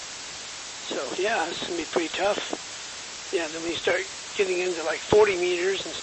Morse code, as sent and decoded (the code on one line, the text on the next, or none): none